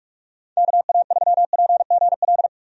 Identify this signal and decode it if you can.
Morse code, heard as a keyed tone, about 45 wpm, 700 Hz. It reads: KA3PGL
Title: USA Call Signs - 6 Characters - 45wpm